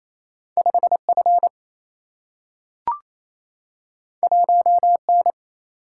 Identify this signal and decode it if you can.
Morse code, heard as a keyed tone, about 28 wpm, 700 Hz. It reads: 5 F 1 N
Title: Single-Letter Single-Number 28wpm